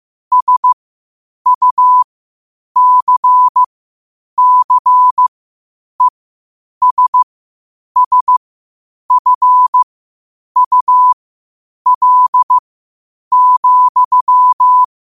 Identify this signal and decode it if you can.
Morse code, heard as a keyed tone, about 15 wpm, spaced as for 8 wpm, 1000 Hz. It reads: SUCCESSFUL,
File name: qotdfiles / 8wpm